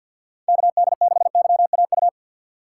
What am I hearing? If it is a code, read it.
Morse code, heard as a keyed tone, about 52 wpm, 700 Hz. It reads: KD6YAU
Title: USA Call Signs - 2x Character Spacing - 50wpm